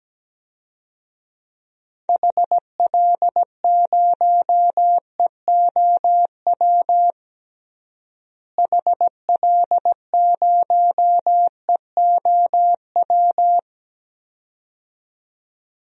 Morse code, heard as a keyed tone, about 17 wpm, 700 Hz. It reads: HL0EOW HL0EOW
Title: DXCC-100 Call Signs - 6 Characters - Repeated 2x - 17wpm